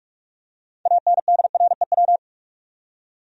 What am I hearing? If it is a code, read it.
Morse code, heard as a keyed tone, about 46 wpm, 700 Hz. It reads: ANDREW
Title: USA Common Names - 45wpm